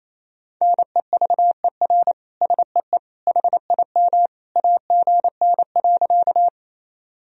Morse code, heard as a keyed tone, about 28 wpm, 700 Hz. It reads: NEVER SEE HIM AGN.
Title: Sentences from Top 300 Words and Abbreviations 28wpm